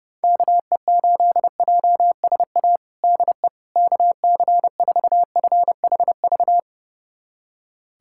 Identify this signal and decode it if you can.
Morse code, heard as a keyed tone, about 32 wpm, 700 Hz. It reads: KE8JSA DE KC4FHV
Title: USA Callsign DE Callsign - 6 Characters - 30wpm